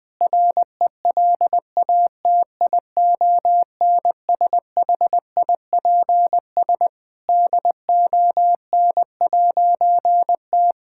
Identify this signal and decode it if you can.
Morse code, heard as a keyed tone, about 21 wpm, 700 Hz. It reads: RELATIONSHIPS DON'T
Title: All-Age Appropriate Jokes - 20wpm